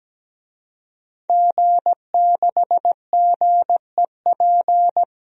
Morse code, heard as a keyed tone, about 17 wpm, 700 Hz. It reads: G6GEP
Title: DXCC-100 Call Signs - 5 Characters - 17wpm